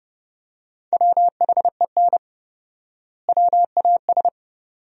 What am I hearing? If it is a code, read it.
Morse code, heard as a keyed tone, about 30 wpm, 700 Hz. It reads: WHEN WAS
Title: 4x Word-Spacing - Sentences from Top 100 Words - 30wpm